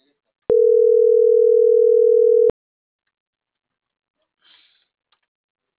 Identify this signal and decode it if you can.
Ringback tone (North America), 440 + 480 Hz.